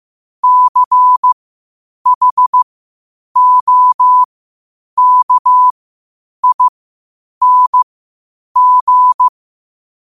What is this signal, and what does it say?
Morse code, heard as a keyed tone, about 15 wpm, spaced as for 8 wpm, 1000 Hz. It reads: CHOKING
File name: qotdfiles / 8wpm